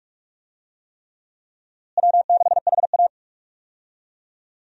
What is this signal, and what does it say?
Morse code, heard as a keyed tone, about 44 wpm, 700 Hz. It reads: W6HA
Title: USA Call Signs - 4 Characters - Repeated 3x - 45wpm